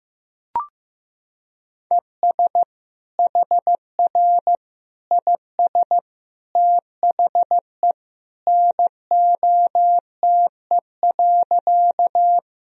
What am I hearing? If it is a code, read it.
Morse code, heard as a keyed tone, about 15 wpm, 700 Hz. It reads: ES HR IS THE NOTE.
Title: Sentences from Top 500 Words and Abbreviations 15wpm